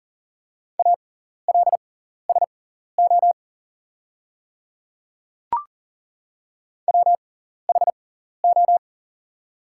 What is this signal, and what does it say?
Morse code, heard as a keyed tone, about 40 wpm, spaced as for 14 wpm, 690 Hz. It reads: ALSO WHO
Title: 6x Character Spacing - Top 200 Words - 40wpm